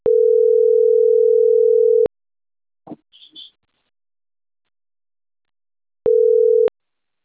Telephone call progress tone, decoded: ringback tone